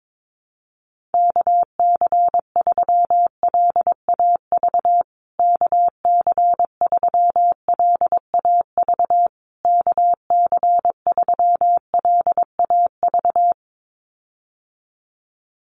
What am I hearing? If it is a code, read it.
Morse code, heard as a keyed tone, about 22 wpm, 690 Hz. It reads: KC3LAV KC3LAV KC3LAV
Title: USA Call Signs Repeated 3x 22wpm